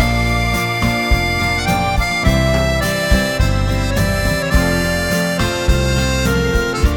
Dial-up modem handshake.